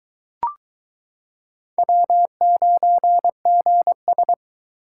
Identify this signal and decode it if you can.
Morse code, heard as a keyed tone, about 22 wpm, 700 Hz. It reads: W9GS